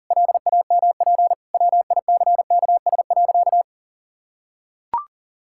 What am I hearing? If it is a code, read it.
Morse code, heard as a keyed tone, about 38 wpm, 700 Hz. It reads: LAMP WICKS.
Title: Pangram Challenge v2 - 40wpm